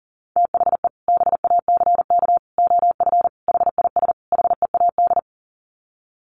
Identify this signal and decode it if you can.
Morse code, heard as a keyed tone, about 40 wpm, 700 Hz. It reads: THE BACK OF HIS HEAD
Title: Sets of 5 Words Encore 40wpm